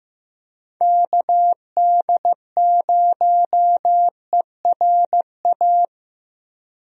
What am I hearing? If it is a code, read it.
Morse code, heard as a keyed tone, about 15 wpm, 700 Hz. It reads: KD0ERA
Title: USA Call Signs - 6 Characters - Repeated 2x - 15wpm